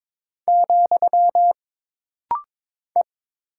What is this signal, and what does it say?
Morse code, heard as a keyed tone, about 22 wpm, 700 Hz. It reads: , E
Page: This rapid-fire timing is geared to push your ICR (Instant Character Recognition) skills.